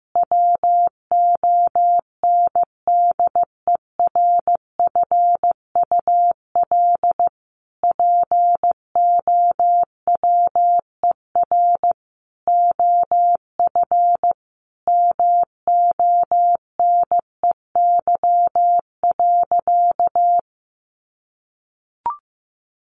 Morse code, heard as a keyed tone, about 14 wpm, 700 Hz. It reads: WONDERFUL POWER OF MONEY.
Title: Sentences from Top 1100 Words 15wpm